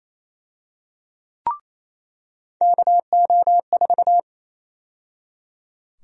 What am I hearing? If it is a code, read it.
Morse code, heard as a keyed tone, about 28 wpm, 700 Hz. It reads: KO4
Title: USA Call Sign Prefixes - 28wpm